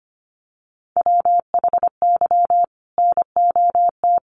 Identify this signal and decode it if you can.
Morse code, heard as a keyed tone, about 25 wpm, 700 Hz. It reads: WHY NOT
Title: Sentences from Top 1500 Words 25wpm